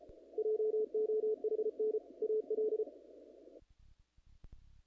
Morse code, heard as a keyed tone, about 31 wpm, 430 Hz. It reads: JOHN AL
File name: contests / naqp-cw-k5tr-op-k5pi- -1900